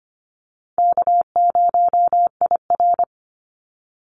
Morse code, heard as a keyed tone, about 25 wpm, 700 Hz. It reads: K0IR
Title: Call Signs - 25wpm